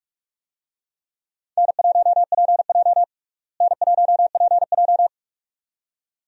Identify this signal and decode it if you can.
Morse code, heard as a keyed tone, about 46 wpm, 700 Hz. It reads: N1PJ N1PJ
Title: USA Call Signs - 4 Characters - Repeated 2x - 45wpm